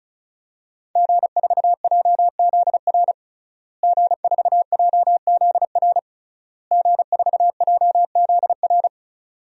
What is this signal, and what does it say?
Morse code, heard as a keyed tone, about 34 wpm, 700 Hz. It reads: G4JZR G4JZR G4JZR
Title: DXCC-100 Call Signs - 5 Characters - Repeated 3x - 35wpm